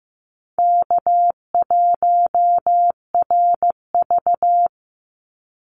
Morse code, heard as a keyed tone, about 15 wpm, 700 Hz. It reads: K1RV
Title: Easy Call Signs - 15wpm